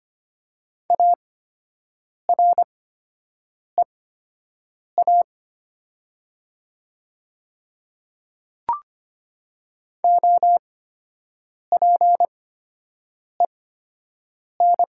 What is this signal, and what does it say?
Morse code, heard as a keyed tone, about 25 wpm, spaced as for 7 wpm, 700 Hz. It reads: AREA OPEN
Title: 8x Character Spacing - Top 500 Words - 25wpm